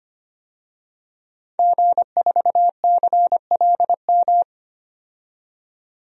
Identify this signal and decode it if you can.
Morse code, heard as a keyed tone, about 25 wpm, 700 Hz. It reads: G4CLM
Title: DXCC-100 Call Signs - 5 Characters - 25wpm